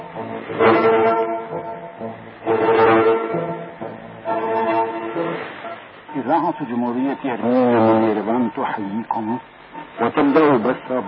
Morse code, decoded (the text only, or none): none